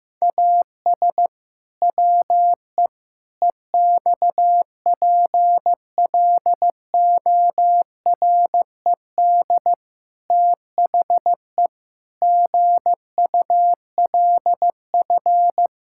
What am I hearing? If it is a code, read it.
Morse code, heard as a keyed tone, about 15 wpm, 700 Hz. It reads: AS WE EXPLORED THE GULF
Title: Pangram Challenge v2 - 15wpm